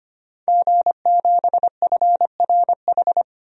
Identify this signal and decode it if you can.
Morse code, heard as a keyed tone, about 25 wpm, 700 Hz. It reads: G7FRH
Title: DXCC-100 Call Signs - 5 Characters - 25wpm